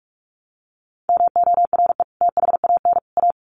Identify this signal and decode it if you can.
Morse code, heard as a keyed tone, about 44 wpm, 700 Hz. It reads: MORE THAN A
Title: Sets of 3 Words 45wpm